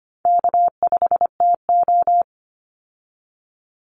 Morse code, heard as a keyed tone, about 25 wpm, 700 Hz. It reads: K5TO
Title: Easy Call Signs Repeated 2x - 25wpm